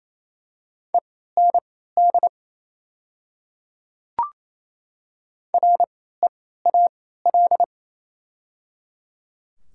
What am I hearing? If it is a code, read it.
Morse code, heard as a keyed tone, about 28 wpm, spaced as for 16 wpm, 700 Hz. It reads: END REAL
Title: 3x Character Spacing - Top 300 Words - 28wpm